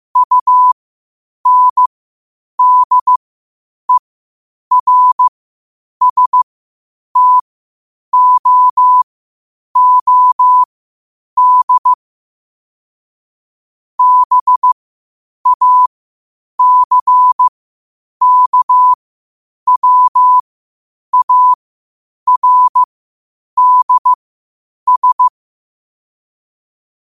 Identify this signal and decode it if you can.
Morse code, heard as a keyed tone, about 15 wpm, spaced as for 8 wpm, 1000 Hz. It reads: UNDERSTOOD BACKWARDS
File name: qotdfiles / 8wpm